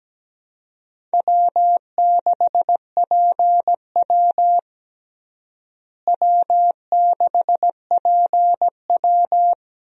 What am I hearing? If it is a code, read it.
Morse code, heard as a keyed tone, about 17 wpm, 700 Hz. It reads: W6PW W6PW
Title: USA Call Signs - 4 Characters - Repeated 2x - 17wpm